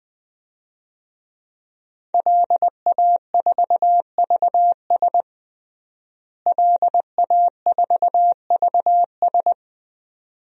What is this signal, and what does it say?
Morse code, heard as a keyed tone, about 20 wpm, 700 Hz. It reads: LA4VS LA4VS
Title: DXCC-100 Call Signs - 5 Characters - Repeated 2x - 20wpm